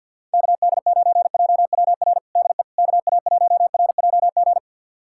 Morse code, heard as a keyed tone, about 46 wpm, 700 Hz. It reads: KD9JWR DE KA1RJG
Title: USA Callsign DE Callsign 50wpm